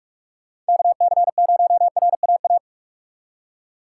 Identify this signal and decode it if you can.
Morse code, heard as a keyed tone, about 60 wpm, spaced as for 52 wpm, 700 Hz. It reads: KC0RAA
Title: Speed Racing - USA 2x3 Call Signs 45wpm